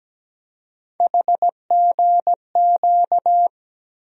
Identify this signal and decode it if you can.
Morse code, heard as a keyed tone, about 17 wpm, 700 Hz. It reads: HGQ
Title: USA Call Sign Postfixes - 3 Letters 17wpm